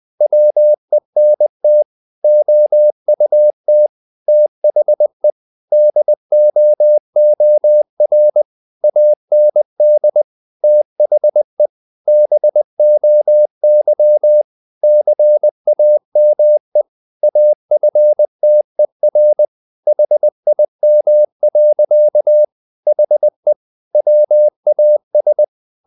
Morse code, heard as a keyed tone, about 20 wpm, 600 Hz. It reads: WENT OUT THE DOOR AND THE BOY CAME AFTER HIM. HE WAS
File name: the old man and the sea / the old man and the sea part 1 20wpm